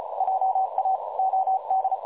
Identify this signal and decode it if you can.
Morse code, heard as a keyed tone, about 33 wpm, 760 Hz, with noise. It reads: JA OD
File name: dlazaawansowanych / nagraniazpasma / websdr recording 3565.0kHz